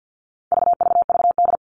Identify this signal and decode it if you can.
Morse code, heard as a keyed tone, about 50 wpm, 700 Hz. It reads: VVVD
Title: All Cut Numbers - 4-Digit 50wpm